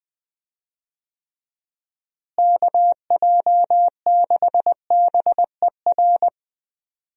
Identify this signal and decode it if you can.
Morse code, heard as a keyed tone, about 20 wpm, 700 Hz. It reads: KJ6BER
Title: USA Call Signs - 6 Characters - Repeated 3x - 20wpm